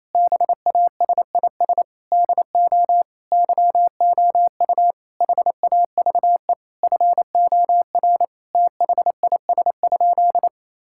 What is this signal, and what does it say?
Morse code, heard as a keyed tone, about 28 wpm, 700 Hz. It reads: BASIS DO YOU HAVE FOR THIS?
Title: Sentences from Top 1000 Words - Review Part 5 - 28wpm